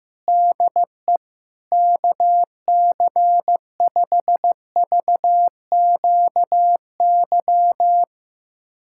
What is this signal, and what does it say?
Morse code, heard as a keyed tone, about 15 wpm, 700 Hz. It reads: DE KC5VQY
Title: USA Callsign DE Callsign - 6 Characters - 15wpm